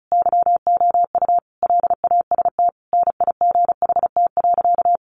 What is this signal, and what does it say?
Morse code, heard as a keyed tone, about 34 wpm, 700 Hz. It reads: YOU LAST NIGHT.
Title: Sentences from Top 1300 Words 35wpm